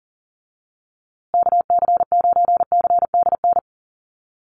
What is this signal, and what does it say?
Morse code, heard as a keyed tone, about 40 wpm, 700 Hz. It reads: KC9CDN
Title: USA Call Signs Repeated 2x 40wpm